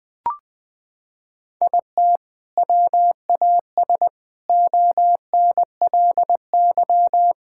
Morse code, heard as a keyed tone, about 21 wpm, 700 Hz. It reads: IT WAS ONLY
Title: Sentences with 8-Word Phrases - 20wpm